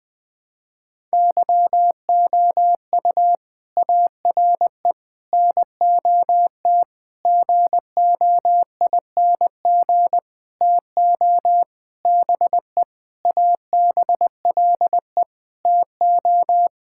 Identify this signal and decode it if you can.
Morse code, heard as a keyed tone, about 21 wpm, 700 Hz. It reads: YOU ARE NOT GOING TO BE ABLE TO
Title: Sentences with 7-Word Phrases - Spoken Beginning - 20wpm